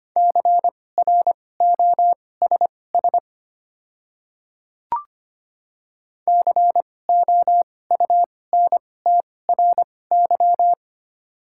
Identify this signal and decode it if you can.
Morse code, heard as a keyed tone, about 25 wpm, spaced as for 18 wpm, 700 Hz. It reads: CROSS COUNTRY
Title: 2x Character Spacing - Top 300 Words - 25wpm